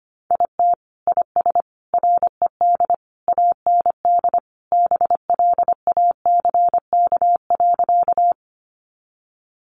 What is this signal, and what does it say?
Morse code, heard as a keyed tone, about 25 wpm, 700 Hz. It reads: IT IS RED AND BLACK.
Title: Additional Easy Phrases 25wpm